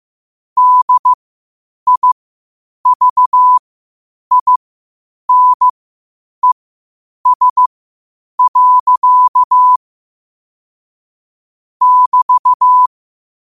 Morse code, heard as a keyed tone, about 15 wpm, spaced as for 8 wpm, 1000 Hz. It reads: DIVINES. <BT>
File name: qotdfiles / 8wpm